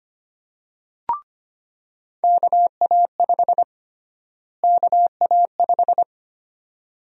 Morse code, heard as a keyed tone, about 25 wpm, 700 Hz. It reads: KA5 KA5
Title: USA Call Sign Prefixes - Repeated 2x - 25wpm